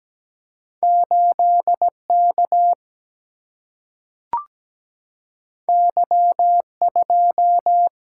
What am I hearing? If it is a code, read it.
Morse code, heard as a keyed tone, about 17 wpm, 700 Hz. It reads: 8 K Y 2
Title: Single-Letter Single-Number 17wpm